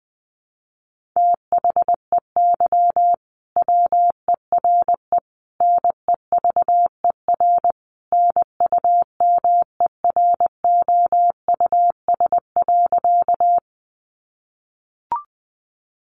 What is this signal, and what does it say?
Morse code, heard as a keyed tone, about 20 wpm, 700 Hz. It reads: THEY WERE NEVER NUMEROUS.
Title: Sentences from Top 1400 Words 20wpm